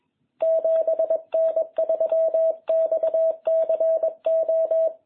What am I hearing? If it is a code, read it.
Morse code, heard as a keyed tone, about 21 wpm, 660 Hz. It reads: 7N3XCO